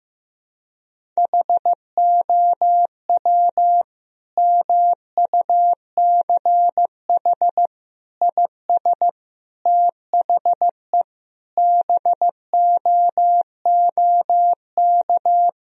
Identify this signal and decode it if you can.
Morse code, heard as a keyed tone, about 15 wpm, 700 Hz. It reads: HOW MUCH IS THE BOOK
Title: Sentences from Top 1000 Words - Review Part 1 - 15wpm